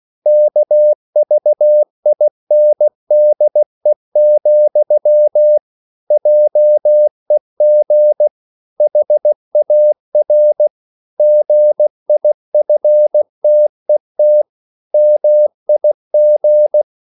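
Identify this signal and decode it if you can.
Morse code, heard as a keyed tone, about 16 wpm, 600 Hz. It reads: KVINDE, JEG HAR GIFTET MIG
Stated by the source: Slow (QRS) Morse code at 16 wpm, from CW med Gnister.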